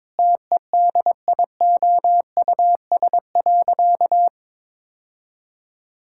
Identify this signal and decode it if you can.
Morse code, heard as a keyed tone, about 22 wpm, 700 Hz. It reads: TEDIOUS.
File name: Sentences with 7-Word Phrases - 22wpm